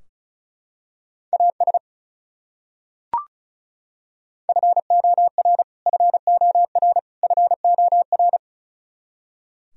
Morse code, heard as a keyed tone, about 36 wpm, 700 Hz. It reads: AS FOR FOR FOR
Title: Top 100 Words Repeated 35wpm